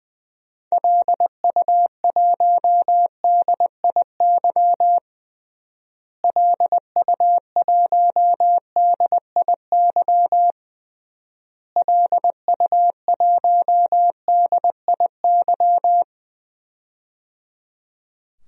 Morse code, heard as a keyed tone, about 20 wpm, 700 Hz. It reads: LU1DIY LU1DIY LU1DIY
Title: DXCC-100 Call Signs - 6 Characters - Repeated 3x - 20wpm